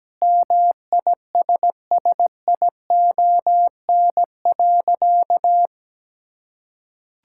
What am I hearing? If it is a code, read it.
Morse code, heard as a keyed tone, about 17 wpm, 700 Hz. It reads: MISSION.
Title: Sentences from Top 1600 Words 17wpm